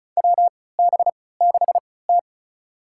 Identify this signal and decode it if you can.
Morse code, heard as a keyed tone, about 36 wpm, spaced as for 20 wpm, 700 Hz. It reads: WB6T
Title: USA Call Signs - 3x Character Spacing - 35wpm